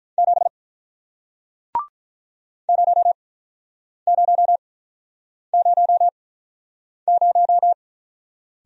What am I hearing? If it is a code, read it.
Morse code, heard as a keyed tone, about 45 wpm, spaced as for 9 wpm, 700 Hz. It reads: 7 0000